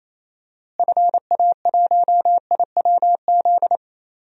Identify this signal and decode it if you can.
Morse code, heard as a keyed tone, about 28 wpm, 700 Hz. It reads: FA1IWZ
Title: DXCC-100 Call Signs - 6 Characters - Repeated 3x - 28wpm